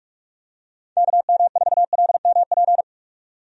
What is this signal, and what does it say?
Morse code, heard as a keyed tone, about 46 wpm, 700 Hz. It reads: KM4LMP
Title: USA Call Signs - 6 Characters - 45wpm